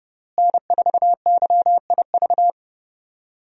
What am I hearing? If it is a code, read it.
Morse code, heard as a keyed tone, about 30 wpm, 700 Hz. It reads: N4YIV
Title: USA Call Signs - 5 Characters - Repeated 2x - 30wpm